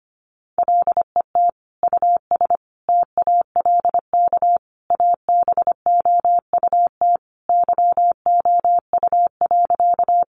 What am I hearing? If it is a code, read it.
Morse code, heard as a keyed tone, about 25 wpm, 700 Hz. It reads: LET US TALK ABOUT YOU.
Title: Sentences from Top 400 Words 25wpm